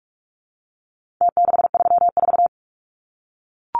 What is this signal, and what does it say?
Morse code, heard as a keyed tone, about 44 wpm, 700 Hz. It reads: T634
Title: Numbers - Contest Style - 4-Digit 45wpm